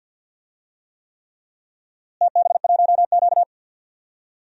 Morse code, heard as a keyed tone, about 51 wpm, 700 Hz. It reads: TB1Q
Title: DXCC-100 Call Signs - 4 Characters - Repeated 3x - 50wpm